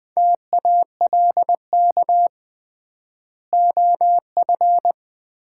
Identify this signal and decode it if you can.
Morse code, heard as a keyed tone, about 20 wpm, 700 Hz. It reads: TALK OF
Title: 3x Word-Spacing - Sets of 5 Words - 20wpm